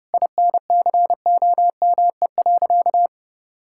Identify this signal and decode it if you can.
Morse code, heard as a keyed tone, about 30 wpm, 700 Hz. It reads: INCOME.
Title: Sentences from Top 1000 Words - Review Part 5 - 30wpm